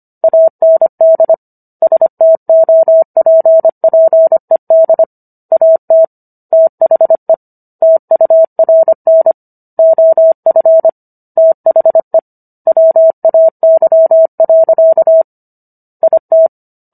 Morse code, heard as a keyed tone, about 25 wpm, 650 Hz. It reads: AND STOPPED AT THE TURN OF THE WAY. IT